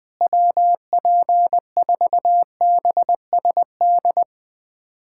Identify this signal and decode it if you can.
Morse code, heard as a keyed tone, about 20 wpm, 700 Hz. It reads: WP4BSD
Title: USA Callsign DE Callsign 20wpm